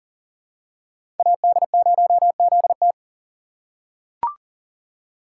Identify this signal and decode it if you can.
Morse code, heard as a keyed tone, about 40 wpm, 700 Hz. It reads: AD0ZT
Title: USA Call Signs - 5 Characters - 4x Character Spacing - 40wpm